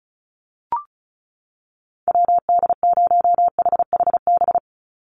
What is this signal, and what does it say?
Morse code, heard as a keyed tone, about 34 wpm, 700 Hz. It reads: WD0HHB
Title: USA Call Signs 35wpm